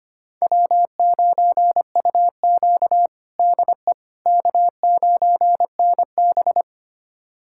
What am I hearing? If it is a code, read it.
Morse code, heard as a keyed tone, about 25 wpm, 700 Hz. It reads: W9UQ DE K9NB
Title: USA Callsign DE Callsign - 4 Characters - 25wpm